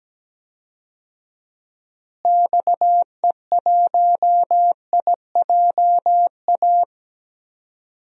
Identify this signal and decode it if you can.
Morse code, heard as a keyed tone, about 17 wpm, 700 Hz. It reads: XE1IJA
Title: DXCC-100 Call Signs - 6 Characters - Repeated 2x - 17wpm